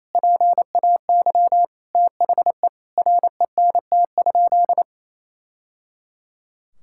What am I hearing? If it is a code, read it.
Morse code, heard as a keyed tone, about 28 wpm, 700 Hz. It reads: PAY THE RENT?
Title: Sentences from Top 1000 Words - Review Part 5 - 28wpm